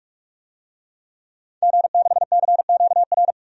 Morse code, heard as a keyed tone, about 45 wpm, 700 Hz. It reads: G6CQR
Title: DXCC-100 Call Signs - 5 Characters - 45wpm